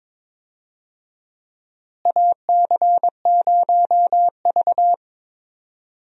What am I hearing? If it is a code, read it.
Morse code, heard as a keyed tone, about 22 wpm, 700 Hz. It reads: AC0V
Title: USA Call Signs - 4 Characters - Repeated 3x - 22wpm